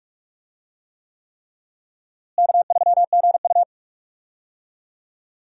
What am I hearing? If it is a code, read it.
Morse code, heard as a keyed tone, about 46 wpm, 700 Hz. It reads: K3GU
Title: USA Call Signs - 4 Characters - Repeated 3x - 45wpm